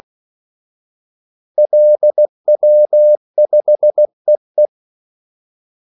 Morse code, heard as a keyed tone, about 16 wpm, 600 Hz. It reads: LW5EE
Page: Kaldesignaler 16 wpm